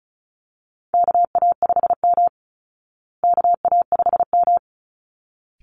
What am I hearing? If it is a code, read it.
Morse code, heard as a keyed tone, about 34 wpm, 700 Hz. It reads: KA5M KA5M
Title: Call Signs Repeated 3x - 35wpm